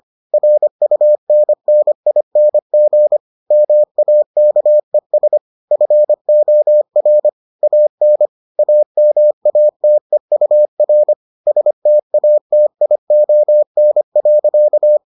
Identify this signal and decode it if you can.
Morse code, heard as a keyed tone, about 25 wpm, 600 Hz. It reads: RUNNING MAKES FOR AN AMATEUR STATION.